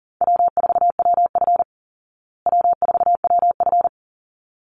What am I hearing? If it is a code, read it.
Morse code, heard as a keyed tone, about 40 wpm, 700 Hz. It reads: W4WF W4WF
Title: Easy Call Signs Repeated 2x - 40wpm